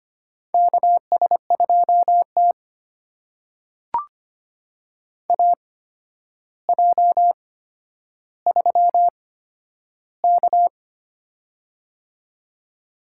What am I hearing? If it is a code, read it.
Morse code, heard as a keyed tone, about 25 wpm, spaced as for 11 wpm, 700 Hz. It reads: KS2T AJ3K
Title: USA Call Signs - 4 Characters - 8x Character Spacing - 25wpm